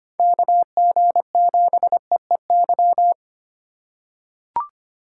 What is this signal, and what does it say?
Morse code, heard as a keyed tone, about 25 wpm, 700 Hz. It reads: KG7EEY
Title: USA Call Signs - 3x Character Spacing - 25wpm